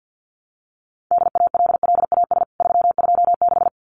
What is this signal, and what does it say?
Morse code, heard as a keyed tone, about 50 wpm, 700 Hz. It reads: DALLAS 326
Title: CWT Exchange 50wpm